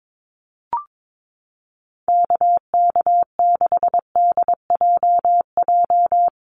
Morse code, heard as a keyed tone, about 22 wpm, 700 Hz. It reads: KK6DJJ
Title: USA Call Signs 22wpm